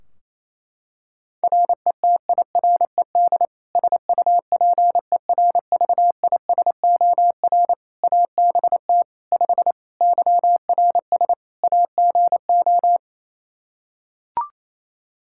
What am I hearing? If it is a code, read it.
Morse code, heard as a keyed tone, about 28 wpm, 700 Hz. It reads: RETIRED SUPERVISOR ABT 5 YRS AGO
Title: Practice QSO - Exchange 3 - More Info - 28wpm